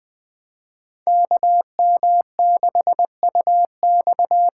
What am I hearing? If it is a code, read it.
Morse code, heard as a keyed tone, about 20 wpm, 700 Hz. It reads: KM6UX
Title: USA Call Signs - 6x Character Spacing - 20wpm